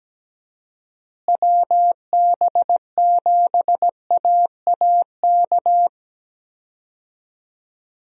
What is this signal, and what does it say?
Morse code, heard as a keyed tone, about 17 wpm, 700 Hz. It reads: WB7AAK
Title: USA Call Signs - 6 Characters - Repeated 2x - 17wpm